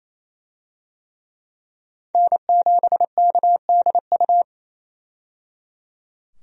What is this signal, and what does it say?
Morse code, heard as a keyed tone, about 28 wpm, 700 Hz. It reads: N7KDU
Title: USA Call Signs - 5 Characters - 28wpm